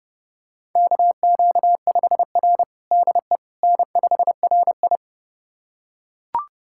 Morse code, heard as a keyed tone, about 30 wpm, 700 Hz. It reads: KQ5R DE N5RI
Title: USA Callsign DE Callsign - 4 Characters - 30wpm